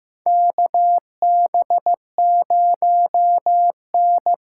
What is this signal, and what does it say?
Morse code, heard as a keyed tone, about 15 wpm, 700 Hz. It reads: KB0N
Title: USA Call Signs - 6x Character Spacing - 15wpm